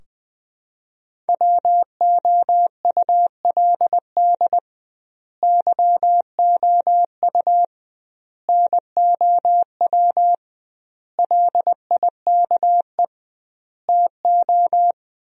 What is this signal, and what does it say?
Morse code, heard as a keyed tone, about 20 wpm, 700 Hz. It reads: WOULD YOU NOW LIKE TO
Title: 2x Word-Spacing - Sentences from Top 200 Words - 20wpm